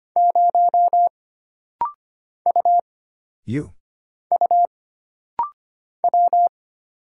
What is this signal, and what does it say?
Morse code, heard as a keyed tone, about 25 wpm, 700 Hz. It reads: 0 U U W
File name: Single Character (Letter:Number:Punctuation:Prosign) - Rapid-Fire - 25wpm